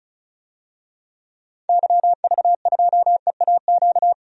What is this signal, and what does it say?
Morse code, heard as a keyed tone, about 36 wpm, 700 Hz. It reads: YV2EAQ
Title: DXCC-100 Call Signs - 6 Characters - 35wpm